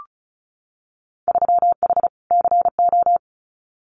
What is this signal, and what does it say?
Morse code, heard as a keyed tone, about 36 wpm, 700 Hz. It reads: 3H CO
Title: Winter Field Day Exchanges 35wpm